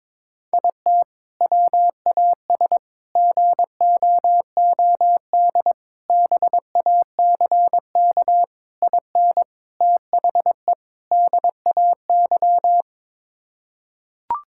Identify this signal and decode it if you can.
Morse code, heard as a keyed tone, about 22 wpm, 700 Hz. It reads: IT WAS GOOD BACK IN THE DAY
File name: practice / 22wpm / Sentences with 3-Word Phrases - 22wpm